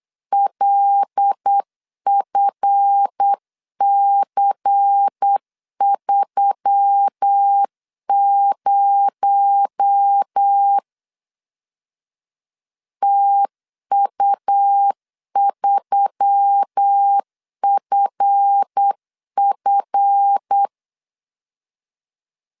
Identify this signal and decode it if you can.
Morse code, heard as a keyed tone, about 8 wpm, 800 Hz. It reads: LFC30 TU3FF
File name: LEZIONI / 7 GF9Z / 7 GF 8wpm 3108